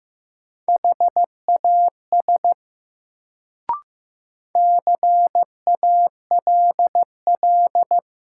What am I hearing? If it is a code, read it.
Morse code, heard as a keyed tone, about 15 wpm, 700 Hz. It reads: HAS CALL
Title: Sentences from Top 100 Words - New Word Drill - 15wpm